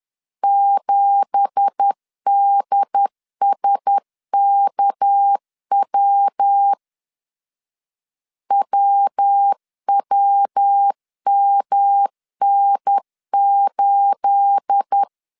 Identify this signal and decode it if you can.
Morse code, heard as a keyed tone, about 10 wpm, 800 Hz. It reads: 7DSKW WWMN8
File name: CW / DET / 1 9 1500C S B 10wpm 4439